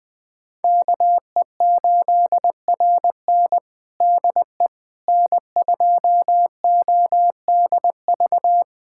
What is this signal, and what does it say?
Morse code, heard as a keyed tone, about 20 wpm, 700 Hz. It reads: KE8RN DE N2ODV
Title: USA Callsign DE Callsign - 5 Characters - 20wpm